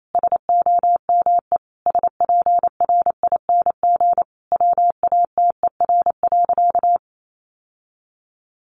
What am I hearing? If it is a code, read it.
Morse code, heard as a keyed tone, about 28 wpm, 700 Hz. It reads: SOME SPRING WATER.
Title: Sentences from Top 1200 Words 28wpm